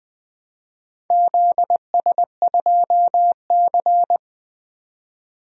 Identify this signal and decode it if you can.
Morse code, heard as a keyed tone, about 20 wpm, 700 Hz. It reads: ZS2C
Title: DXCC-100 Call Signs - 4 Characters - 20wpm